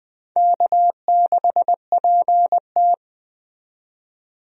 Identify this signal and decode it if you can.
Morse code, heard as a keyed tone, about 20 wpm, 700 Hz. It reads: K6PT
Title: USA Call Signs - 4 Characters - 20wpm